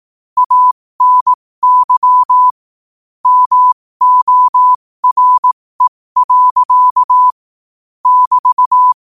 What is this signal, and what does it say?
Morse code, heard as a keyed tone, about 18 wpm, spaced as for 15 wpm, 1000 Hz. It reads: ANY MORE. <BT>
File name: qotdfiles / 15wpm